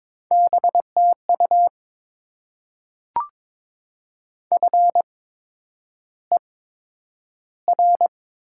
Morse code, heard as a keyed tone, about 22 wpm, spaced as for 9 wpm, 700 Hz. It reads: BTU FER
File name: Top 20 QSO Elements - 8x Character Spacing - 22wpm